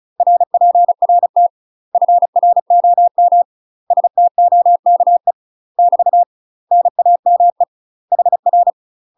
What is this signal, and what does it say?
Morse code, heard as a keyed tone, about 34 wpm, 700 Hz. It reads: RPRT FROM STOKE <BT> NAME HR
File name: UK MASTER QSOs TRAINING / 35WPM / UK MASTER QSOs 0043